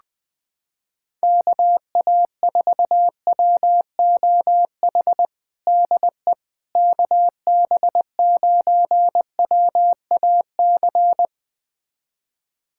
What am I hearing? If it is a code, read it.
Morse code, heard as a keyed tone, about 20 wpm, 700 Hz. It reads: KA4WOH DE KB9WAC
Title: USA Callsign DE Callsign 20wpm